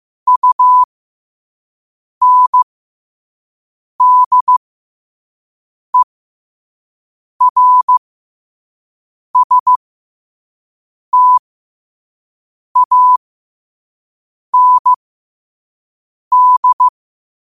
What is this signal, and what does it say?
Morse code, heard as a keyed tone, about 15 wpm, spaced as for 5 wpm, 1000 Hz. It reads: UNDERSTAND
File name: qotdfiles / 5wpm